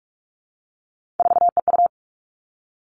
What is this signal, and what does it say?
Morse code, heard as a keyed tone, about 46 wpm, 700 Hz. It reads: 4EU
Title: Three Challenging Letter-Number Combinations 45wpm